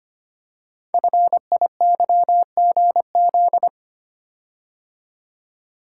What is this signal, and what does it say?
Morse code, heard as a keyed tone, about 25 wpm, 700 Hz. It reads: FIYGZ
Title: Code Groups - 5 Letters - 25wpm